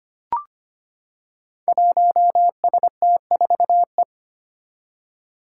Morse code, heard as a keyed tone, about 25 wpm, 700 Hz. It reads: 1ST4E
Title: Five Challenging Letter-Number Combinations 25wpm